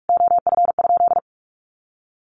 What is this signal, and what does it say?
Morse code, heard as a keyed tone, about 44 wpm, 700 Hz. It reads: OF?
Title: Sentences from Top 1400 Words 45wpm